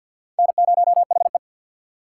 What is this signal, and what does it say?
Morse code, heard as a keyed tone, about 53 wpm, 700 Hz. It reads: N0HE
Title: USA Call Signs - 4 Characters - 3x Character Spacing - 50wpm